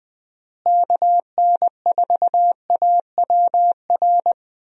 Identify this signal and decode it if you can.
Morse code, heard as a keyed tone, about 20 wpm, 700 Hz. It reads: KN4AWR
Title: USA Call Signs - 6 Characters - Repeated 2x - 20wpm